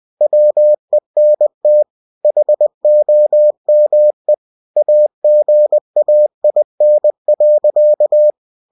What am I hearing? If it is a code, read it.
Morse code, heard as a keyed tone, about 20 wpm, 600 Hz. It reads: WENT HOME AGAIN.